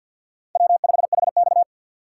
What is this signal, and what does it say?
Morse code, heard as a keyed tone, about 50 wpm, 700 Hz. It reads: W5HX
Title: USA Call Signs - 4 Characters - 3x Character Spacing - 50wpm